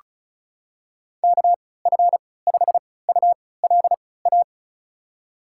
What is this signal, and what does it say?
Morse code, heard as a keyed tone, about 34 wpm, spaced as for 20 wpm, 700 Hz. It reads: KF5ULA
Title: USA Call Signs - 6 Characters - 3x Character Spacing - 35wpm